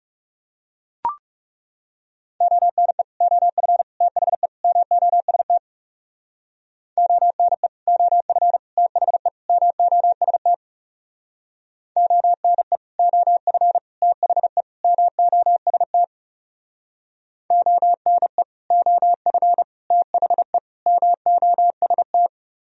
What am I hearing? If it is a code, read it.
Morse code, heard as a keyed tone, about 40 wpm, 700 Hz. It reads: ONE OF THE MOST ONE OF THE MOST ONE OF THE MOST ONE OF THE MOST